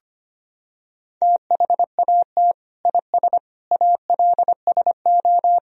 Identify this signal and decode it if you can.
Morse code, heard as a keyed tone, about 25 wpm, 700 Hz. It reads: THAT IS ALSO
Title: Sentences from Top 1000 Words - Review Part 1 - 25wpm